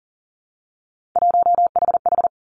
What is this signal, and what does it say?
Morse code, heard as a keyed tone, about 41 wpm, 700 Hz. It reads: 1HH
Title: Three Challenging Letter-Number Combinations 40wpm